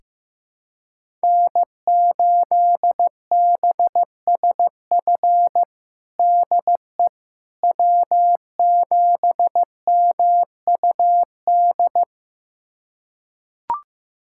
Morse code, heard as a keyed tone, about 15 wpm, 700 Hz. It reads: N8BSF DE W7MUD
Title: USA Callsign DE Callsign - 5 Characters - 15wpm